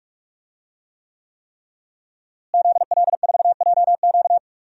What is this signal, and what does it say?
Morse code, heard as a keyed tone, about 45 wpm, 700 Hz. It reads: ZL4JQ
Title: DXCC-100 Call Signs - 5 Characters - Repeated 3x - 45wpm